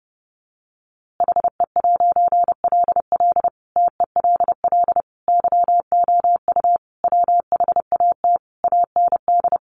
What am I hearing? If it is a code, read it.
Morse code, heard as a keyed tone, about 30 wpm, 700 Hz. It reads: HE'LL TELL YOU WHAT AND
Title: A Sound Of Thunder 30wpm